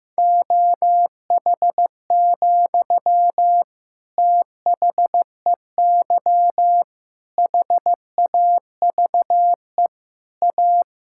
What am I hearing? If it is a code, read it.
Morse code, heard as a keyed tone, about 16 wpm, 700 Hz. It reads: OH, THEY HAVE A
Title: Sentences from Top 1000 Words - Review Part 5 - 15wpm